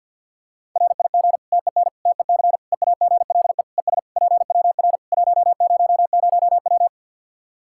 Morse code, heard as a keyed tone, about 50 wpm, 700 Hz. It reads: RIG TEN TEC EAGLE ES PWR 100W
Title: Practice QSO - Exchange 2 - Info - 50wpm